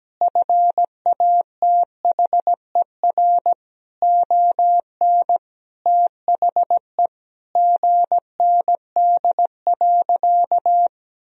Morse code, heard as a keyed tone, about 17 wpm, 700 Hz. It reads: FATHER ON THE GND.
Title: Sentences from Top 500 Words and Abbreviations 17wpm